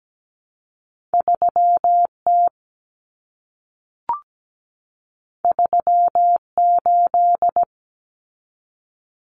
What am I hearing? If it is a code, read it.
Morse code, heard as a keyed tone, about 17 wpm, 700 Hz. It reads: 3T 38
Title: Common Cut Numbers - 2-Digit 17wpm